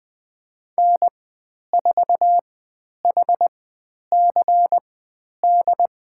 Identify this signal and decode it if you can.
Morse code, heard as a keyed tone, about 20 wpm, spaced as for 10 wpm, 700 Hz. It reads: N4HCD
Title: 109 Lesson 18g - Letter m - Call Signs 20wpm